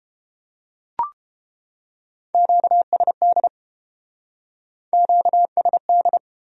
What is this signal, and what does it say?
Morse code, heard as a keyed tone, about 32 wpm, 700 Hz. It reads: QSD QSD